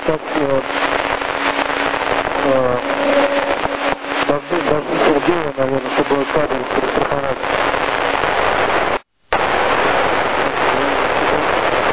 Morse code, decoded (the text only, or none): none